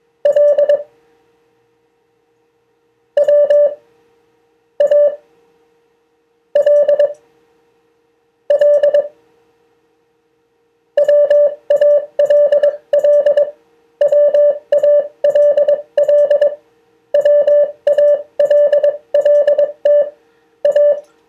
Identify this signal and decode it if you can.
Morse code, heard as a keyed tone, about 22 wpm, spaced as for 19 wpm, 580 Hz. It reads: L W A L L WALL WALL WALLT A
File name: 2of4-words-1